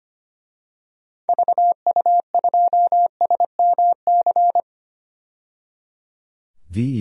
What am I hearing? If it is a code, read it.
Morse code, heard as a keyed tone, about 25 wpm, 700 Hz. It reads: VU2SMC
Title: DXCC-100 Call Signs - 6 Characters - 25wpm